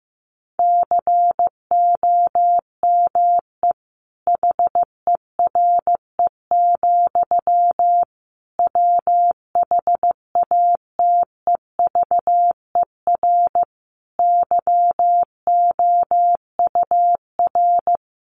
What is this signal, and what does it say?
Morse code, heard as a keyed tone, about 15 wpm, 700 Hz. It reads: COME HERE, WHATEVER YOUR
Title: Sentences from Top 1100 Words 15wpm